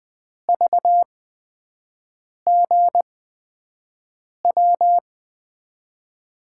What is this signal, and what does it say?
Morse code, heard as a keyed tone, about 20 wpm, spaced as for 5 wpm, 700 Hz. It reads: VGW
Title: USA Call Signs - 8x Character Spacing - 20wpm